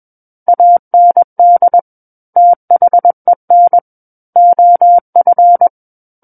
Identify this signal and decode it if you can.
Morse code, heard as a keyed tone, about 21 wpm, 700 Hz. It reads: AND THEN OF
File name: Book One / 12-Chapter 12 - 21 WPM